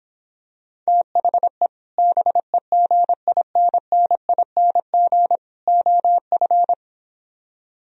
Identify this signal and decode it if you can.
Morse code, heard as a keyed tone, about 26 wpm, 700 Hz. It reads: THE BEGINNING OF